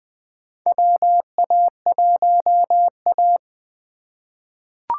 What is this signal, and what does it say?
Morse code, heard as a keyed tone, about 20 wpm, 700 Hz. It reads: WA1A
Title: USA Call Signs - 4 Characters - 4x Character Spacing - 20wpm